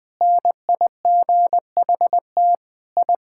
Morse code, heard as a keyed tone, about 20 wpm, 700 Hz. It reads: NIGHT I
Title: Sentences with 4-Word Phrases - Spoken Beginning - 20wpm